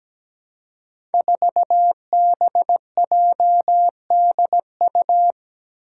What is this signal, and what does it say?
Morse code, heard as a keyed tone, about 17 wpm, 700 Hz. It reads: 4BJDU
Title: Five Challenging Letter-Number Combinations 17wpm